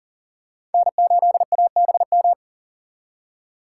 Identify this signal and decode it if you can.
Morse code, heard as a keyed tone, about 40 wpm, 700 Hz. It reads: N8ABM
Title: USA Call Signs - 2x Character Spacing - 40wpm